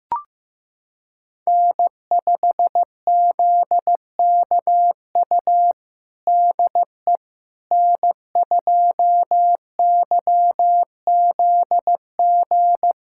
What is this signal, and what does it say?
Morse code, heard as a keyed tone, about 15 wpm, 700 Hz. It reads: N5ZKU DE N2YZG
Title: USA Callsign DE Callsign - 5 Characters - 15wpm